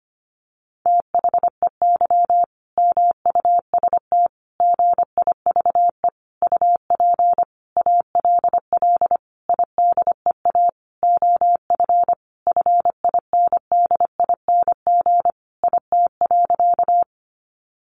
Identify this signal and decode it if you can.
Morse code, heard as a keyed tone, about 25 wpm, 700 Hz. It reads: THEY MUST GIVE UP ALL IDEA OF FINDING IT.
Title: Sentences from Top 1100 Words 25wpm